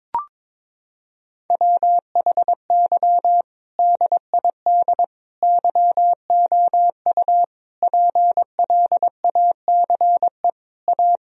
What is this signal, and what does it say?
Morse code, heard as a keyed tone, about 22 wpm, 700 Hz. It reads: WHY DID YOU PLACE A
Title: Sentences from Top 1000 Words - Review Part 4 - 22wpm